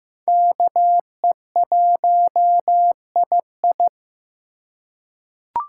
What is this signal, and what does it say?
Morse code, heard as a keyed tone, about 15 wpm, 700 Hz. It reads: KE1II
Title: USA Call Signs - 4x Character Spacing - 15wpm